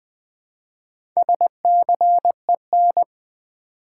Morse code, heard as a keyed tone, about 21 wpm, 700 Hz. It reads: SCEN
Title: Build A Word - 1 Letter At A Time - 5-Letter Words - 20wpm